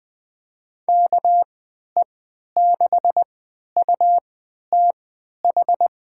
Morse code, heard as a keyed tone, about 20 wpm, spaced as for 11 wpm, 700 Hz. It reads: KE6UTH
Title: USA Call Signs - 6 Characters - 3x Character Spacing - 20wpm